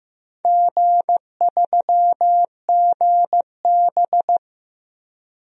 Morse code, heard as a keyed tone, about 15 wpm, 700 Hz. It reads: G3GB
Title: DXCC-100 Call Signs - 4 Characters - 15wpm